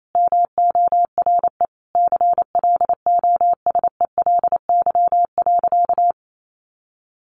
Morse code, heard as a keyed tone, about 28 wpm, 700 Hz. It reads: MORE CLOSELY.
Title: Sentences from Top 1400 Words 28wpm